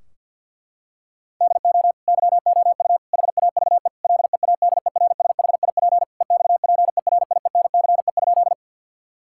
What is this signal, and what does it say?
Morse code, heard as a keyed tone, about 50 wpm, 700 Hz. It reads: DO YOU HAVE LEADERSHIP EXPERIENCE?
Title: Sentences from Top 1600 Words 50wpm